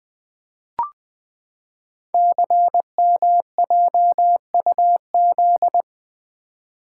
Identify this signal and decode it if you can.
Morse code, heard as a keyed tone, about 20 wpm, 700 Hz. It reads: CMJUZ
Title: Code Groups - 5 Letters - 20wpm